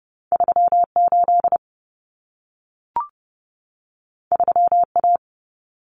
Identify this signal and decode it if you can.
Morse code, heard as a keyed tone, about 30 wpm, 700 Hz. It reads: 38 3A
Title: Common Cut Numbers - 2-Digit 30wpm